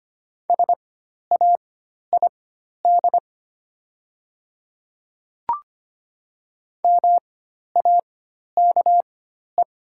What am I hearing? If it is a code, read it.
Morse code, heard as a keyed tone, about 25 wpm, spaced as for 12 wpm, 700 Hz. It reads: SAID MAKE
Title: 4x Character Spacing - Top 200 Words - 25wpm